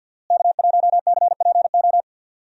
Morse code, heard as a keyed tone, about 49 wpm, 700 Hz. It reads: K1CPO
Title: USA Call Signs - 5 Characters - Repeated 2x - 50wpm